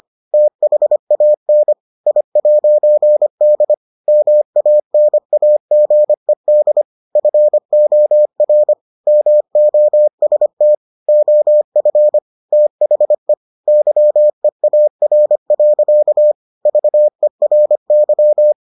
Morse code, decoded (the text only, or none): THAN I'D MANAGED FOR MOST OF THE YEAR. VERY